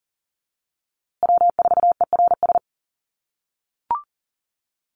Morse code, heard as a keyed tone, about 40 wpm, 700 Hz. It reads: W4ERS
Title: USA Call Signs Repeated 2x 40wpm